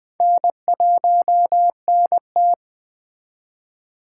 Morse code, heard as a keyed tone, about 21 wpm, 700 Hz. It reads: N1NT